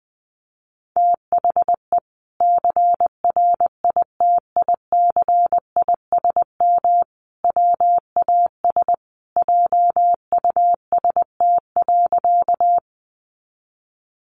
Morse code, heard as a keyed tone, about 20 wpm, 700 Hz. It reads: THE CRITICISM WAS JUST.
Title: Sentences from Top 1300 Words 20wpm